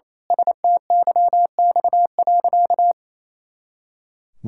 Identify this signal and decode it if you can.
Morse code, heard as a keyed tone, about 28 wpm, 700 Hz. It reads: STYX.
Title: Pangram Challenge v2 - 28wpm